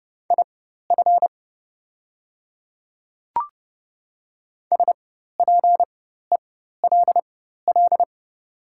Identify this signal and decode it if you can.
Morse code, heard as a keyed tone, about 30 wpm, spaced as for 14 wpm, 700 Hz. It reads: IF SPELL
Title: 4x Character Spacing - Top 200 Words - 30wpm